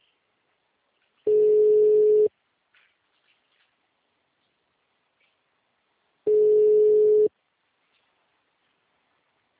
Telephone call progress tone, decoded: ringback tone

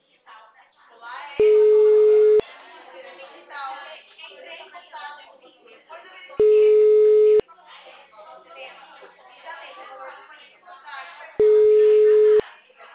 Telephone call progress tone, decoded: ringback tone

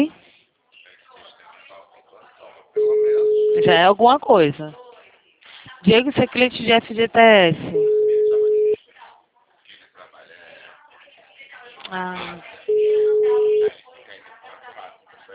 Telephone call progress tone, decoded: ringback tone

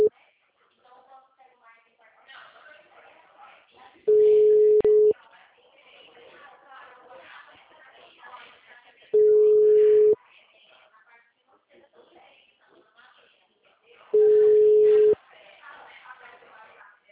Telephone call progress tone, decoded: ringback tone

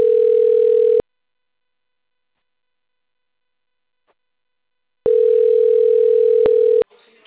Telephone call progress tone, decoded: ringback tone

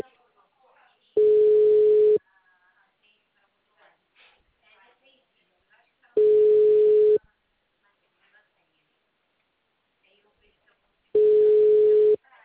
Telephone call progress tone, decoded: ringback tone

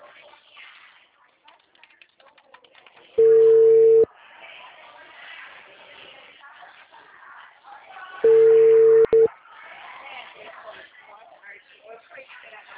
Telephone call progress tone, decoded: ringback tone

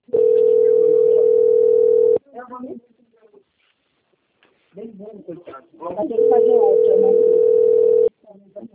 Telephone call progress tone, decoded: ringback tone

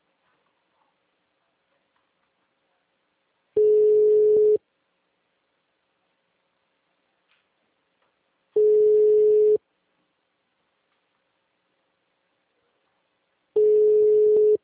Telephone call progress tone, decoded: ringback tone

